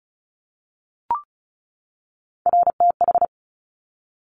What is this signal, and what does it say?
Morse code, heard as a keyed tone, about 36 wpm, 700 Hz. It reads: RTH
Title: Top Three Letter Combinations in Words 35wpm